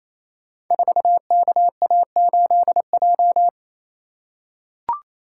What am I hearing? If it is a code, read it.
Morse code, heard as a keyed tone, about 28 wpm, 700 Hz. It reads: 4KA8J
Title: Code Groups - 5 Letter:Numbers - 28wpm